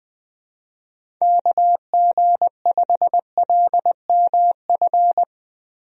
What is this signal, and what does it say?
Morse code, heard as a keyed tone, about 21 wpm, 700 Hz. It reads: KG5LMF
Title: USA Call Signs - 6 Characters - Repeated 2x - 20wpm